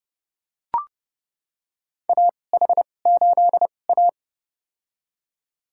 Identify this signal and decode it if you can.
Morse code, heard as a keyed tone, about 30 wpm, spaced as for 22 wpm, 700 Hz. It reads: AH8A
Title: USA Call Signs - 4 Characters - 2x Character Spacing - 30wpm